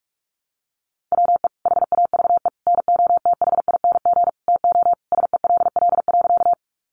Morse code, heard as a keyed tone, about 44 wpm, 700 Hz. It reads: WE HAVE NOTHING TO SELL.
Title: Sentences from Top 1100 Words 45wpm